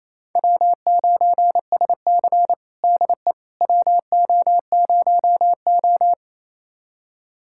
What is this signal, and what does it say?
Morse code, heard as a keyed tone, about 28 wpm, 700 Hz. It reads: W9SC DE WO0O
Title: USA Callsign DE Callsign - 4 Characters - 28wpm